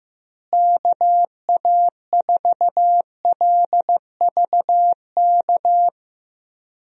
Morse code, heard as a keyed tone, about 15 wpm, 700 Hz. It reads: KA4LVK
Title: USA Call Signs - 6 Characters - Repeated 3x - 15wpm